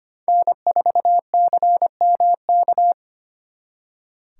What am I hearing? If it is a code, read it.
Morse code, heard as a keyed tone, about 32 wpm, 700 Hz. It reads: N4CMK